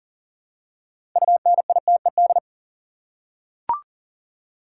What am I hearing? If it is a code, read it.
Morse code, heard as a keyed tone, about 41 wpm, 700 Hz. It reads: UNITED
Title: Six Letter Words - 40wpm